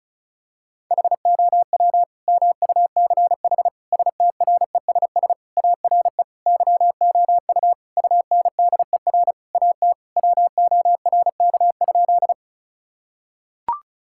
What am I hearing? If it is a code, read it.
Morse code, heard as a keyed tone, about 34 wpm, 700 Hz. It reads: HOW MUCH STRESS ARE YOU UNDER AT WORK?
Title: Sentences from Top 1000 Words - Review Part 6 - 35wpm